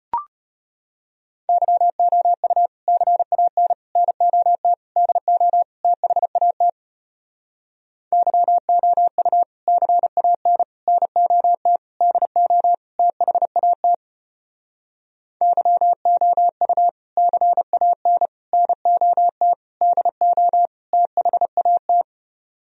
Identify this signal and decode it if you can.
Morse code, heard as a keyed tone, about 33 wpm, 700 Hz. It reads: YOU CAN NOT DO THAT YOU CAN NOT DO THAT YOU CAN NOT DO THAT